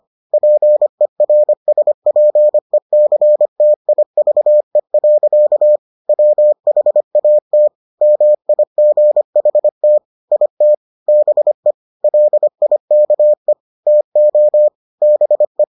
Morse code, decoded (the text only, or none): PERSPECTIVE. WHAT MIGHT IT BE LIKE TO BE